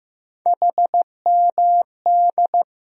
Morse code, heard as a keyed tone, about 16 wpm, 700 Hz. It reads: HMD
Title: USA Call Sign Postfixes - 3 Letters 15wpm